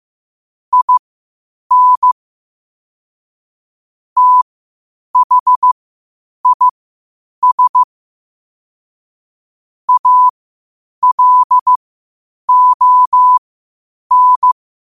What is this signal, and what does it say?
Morse code, heard as a keyed tone, about 15 wpm, spaced as for 8 wpm, 1000 Hz. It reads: IN THIS ALON
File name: qotdfiles / 8wpm